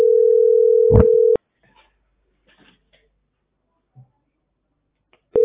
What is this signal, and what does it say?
Ringback tone (North America), 440 + 480 Hz.